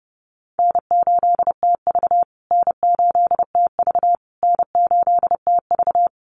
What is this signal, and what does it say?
Morse code, heard as a keyed tone, about 30 wpm, 700 Hz. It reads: N8TV N8TV N8TV
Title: USA Call Signs Repeated 3x 30wpm